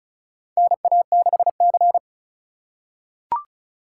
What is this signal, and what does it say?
Morse code, heard as a keyed tone, about 36 wpm, 700 Hz. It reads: NA6C
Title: USA Call Signs - 4 Characters - 4x Character Spacing - 35wpm